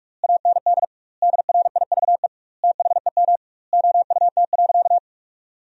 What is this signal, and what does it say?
Morse code, heard as a keyed tone, about 42 wpm, 700 Hz. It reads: AND DRIVE THEM OUT.
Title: 2x Word-Spacing - Sentences from Top 500 Words - 45wpm